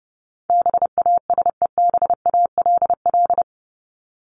Morse code, heard as a keyed tone, about 30 wpm, 700 Hz. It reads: BASEBALL
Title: Top 1751-2000 Words 30wpm